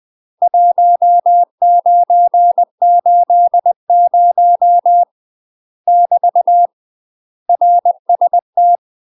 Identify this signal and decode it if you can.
Morse code, heard as a keyed tone, about 20 wpm, 700 Hz. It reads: 1980 <BT> RST
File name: UK MASTER QSOs TRAINING/20WPM